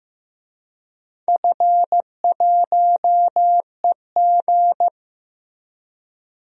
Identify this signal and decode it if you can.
Morse code, heard as a keyed tone, about 16 wpm, 700 Hz. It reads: F1EG
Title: DXCC-100 Call Signs - 4 Characters - Repeated 2x - 15wpm